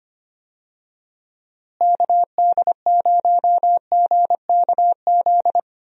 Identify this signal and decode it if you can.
Morse code, heard as a keyed tone, about 25 wpm, 700 Hz. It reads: KD0GKZ
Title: USA Call Signs - 6 Characters - Repeated 3x - 25wpm